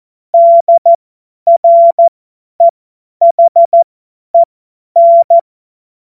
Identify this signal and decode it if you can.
Morse code, heard as a keyed tone, about 14 wpm, spaced as for 10 wpm, 680 Hz. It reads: DREHEN
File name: Witz-22-Juli-2025-50-F70